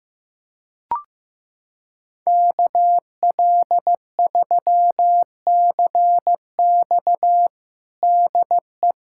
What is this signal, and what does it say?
Morse code, heard as a keyed tone, about 16 wpm, 700 Hz. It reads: KL3CX DE
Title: USA Callsign DE Callsign 15wpm